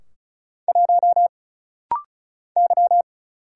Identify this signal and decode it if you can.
Morse code, heard as a keyed tone, about 36 wpm, 700 Hz. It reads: 1 Y
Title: Single Letter-Number Alt - Rapid-Fire 35wpm